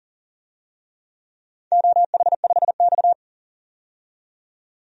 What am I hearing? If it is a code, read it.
Morse code, heard as a keyed tone, about 40 wpm, 700 Hz. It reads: OH5X
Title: DXCC-100 Call Signs - 4 Characters - 40wpm